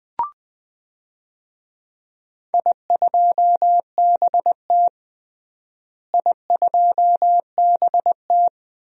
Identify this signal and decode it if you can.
Morse code, heard as a keyed tone, about 20 wpm, 700 Hz. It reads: I2BT I2BT
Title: DXCC-100 Call Signs - 4 Characters - Repeated 2x - 20wpm